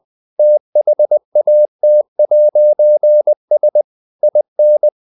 Morse code, heard as a keyed tone, about 20 wpm, 600 Hz. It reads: THAT'S IN
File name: CW / Practice / If20